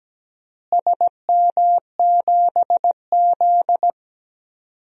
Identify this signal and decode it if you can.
Morse code, heard as a keyed tone, about 17 wpm, 700 Hz. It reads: SM7Z
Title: DXCC-100 Call Signs - 4 Characters - Repeated 3x - 17wpm